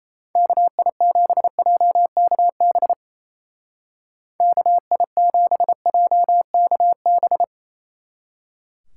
Morse code, heard as a keyed tone, about 37 wpm, 700 Hz. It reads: KI7JKB KI7JKB